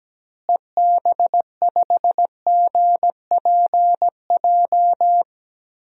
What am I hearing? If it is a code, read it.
Morse code, heard as a keyed tone, about 17 wpm, 700 Hz. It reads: EB5GPJ
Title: DXCC-100 Call Signs - 6 Characters - Repeated 2x - 17wpm